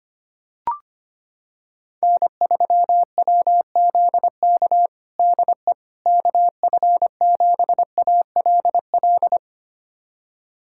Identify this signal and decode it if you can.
Morse code, heard as a keyed tone, about 25 wpm, 700 Hz. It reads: N3WZK DE KF7ALL
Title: USA Callsign DE Callsign 25wpm